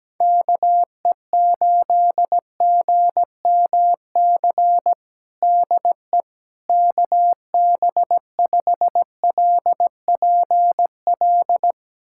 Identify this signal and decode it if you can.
Morse code, heard as a keyed tone, about 17 wpm, 700 Hz. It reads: KE8GMC DE KB5LPL
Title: USA Callsign DE Callsign - 6 Characters - 17wpm